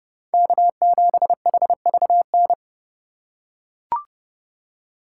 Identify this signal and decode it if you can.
Morse code, heard as a keyed tone, about 29 wpm, 700 Hz. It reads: K7HVN
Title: USA Call Signs - 2x Character Spacing - 30wpm